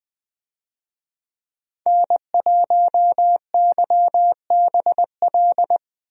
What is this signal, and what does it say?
Morse code, heard as a keyed tone, about 20 wpm, 700 Hz. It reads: N1YBL
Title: USA Call Signs - 5 Characters - Repeated 2x - 20wpm